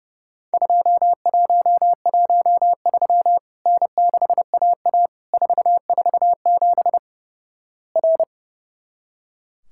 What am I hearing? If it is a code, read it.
Morse code, heard as a keyed tone, about 31 wpm, 700 Hz. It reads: 2113 N6AA 447 R
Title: Worked All Europe DX Contest - QTC Exchange - 30wpm